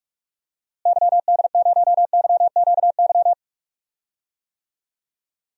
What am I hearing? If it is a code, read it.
Morse code, heard as a keyed tone, about 44 wpm, 700 Hz. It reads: YD0YQY
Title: DXCC-100 Call Signs - 6 Characters - Repeated 2x - 45wpm